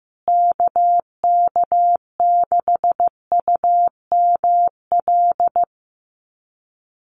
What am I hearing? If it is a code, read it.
Morse code, heard as a keyed tone, about 15 wpm, 700 Hz. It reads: KK6UML
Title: USA Call Signs Repeated 3x 15wpm